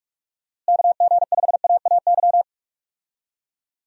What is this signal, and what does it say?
Morse code, heard as a keyed tone, about 46 wpm, 700 Hz. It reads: KG5AAY
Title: USA Call Signs - 6 Characters - Repeated 3x - 45wpm